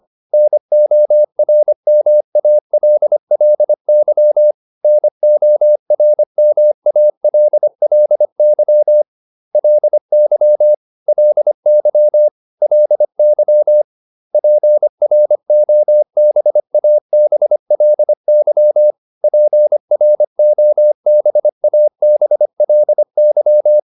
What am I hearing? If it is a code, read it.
Morse code, heard as a keyed tone, about 25 wpm, 600 Hz. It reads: NORMALLY NORMALLY LY LY LY PROBABLY PROBABLY
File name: LY8 25